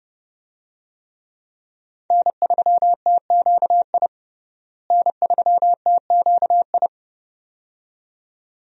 Morse code, heard as a keyed tone, about 30 wpm, 700 Hz. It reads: N3TQI N3TQI
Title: USA Call Signs - 5 Characters - Repeated 2x - 30wpm